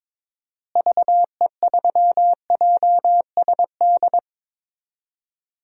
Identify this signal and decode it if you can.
Morse code, heard as a keyed tone, about 21 wpm, 700 Hz. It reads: VE3JSD
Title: DXCC-100 Call Signs - 6 Characters - Repeated 3x - 22wpm